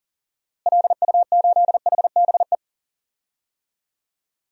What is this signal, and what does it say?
Morse code, heard as a keyed tone, about 37 wpm, 700 Hz. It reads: LU8HBE
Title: DXCC-100 Call Signs - 6 Characters - Repeated 3x - 40wpm